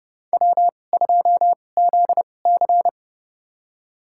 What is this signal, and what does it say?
Morse code, heard as a keyed tone, about 30 wpm, spaced as for 22 wpm, 700 Hz. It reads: W2ZC
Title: USA Call Signs - 4 Characters - 2x Character Spacing - 30wpm